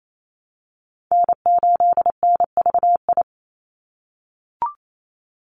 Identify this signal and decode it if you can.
Morse code, heard as a keyed tone, about 28 wpm, 700 Hz. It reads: N8NVI
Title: USA Call Signs 28wpm